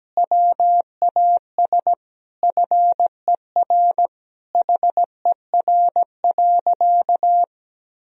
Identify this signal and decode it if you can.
Morse code, heard as a keyed tone, about 17 wpm, 700 Hz. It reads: WAS FER HER.
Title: Sentences from Top 400 Words and Abbreviations 17wpm